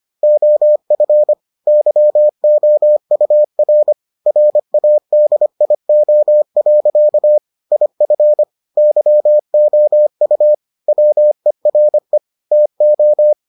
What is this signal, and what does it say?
Morse code, heard as a keyed tone, about 25 wpm, 600 Hz. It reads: OF YOUR RADIO. IF YOU WERE TO